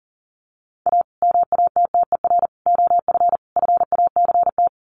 Morse code, heard as a keyed tone, about 40 wpm, 700 Hz. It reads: A MATTER OF FACT
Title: Sets of 4 Words 40wpm